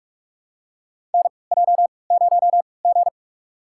Morse code, heard as a keyed tone, about 46 wpm, spaced as for 25 wpm, 700 Hz. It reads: NJ0G
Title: USA Call Signs - 4 Characters - 3x Character Spacing - 45wpm